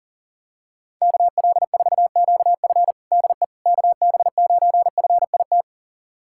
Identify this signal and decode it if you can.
Morse code, heard as a keyed tone, about 37 wpm, 700 Hz. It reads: KL4QF DE KB9FIT
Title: USA Callsign DE Callsign 40wpm